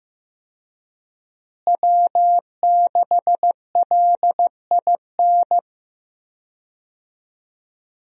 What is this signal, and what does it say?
Morse code, heard as a keyed tone, about 15 wpm, 700 Hz. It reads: W6LIN
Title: USA Call Signs - 5 Characters - Repeated 2x - 15wpm